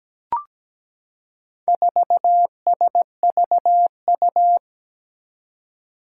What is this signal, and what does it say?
Morse code, heard as a keyed tone, about 17 wpm, 700 Hz. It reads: 4SVU
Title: Four Challenging Letter-Number Combinations 17wpm